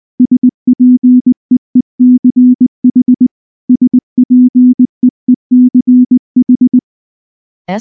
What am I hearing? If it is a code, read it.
Morse code, heard as a keyed tone, about 19 wpm, 250 Hz. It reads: SPEECH SPEECH
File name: morse / morse-training-20WPM-10